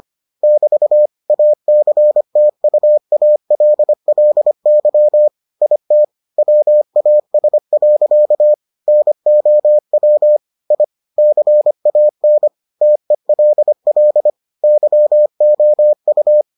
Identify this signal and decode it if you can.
Morse code, heard as a keyed tone, about 25 wpm, 600 Hz. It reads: <BT> ACTUALLY IT WAS. NOW I CAN TELL YOU